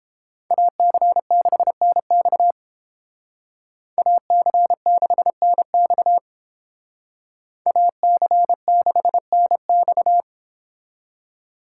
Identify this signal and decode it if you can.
Morse code, heard as a keyed tone, about 31 wpm, 700 Hz. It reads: AC6NX AC6NX AC6NX